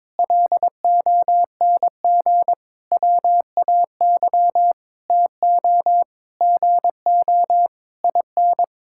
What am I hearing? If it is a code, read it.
Morse code, heard as a keyed tone, about 22 wpm, 700 Hz. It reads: LONG WAY TO GO IN
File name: Sentences with 9-Word Phrases - 22wpm